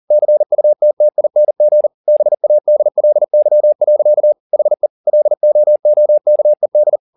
Morse code, heard as a keyed tone, about 40 wpm, 600 Hz. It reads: CUTTING BADLY. HE LOOKED
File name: the old man and the sea 2 40wpm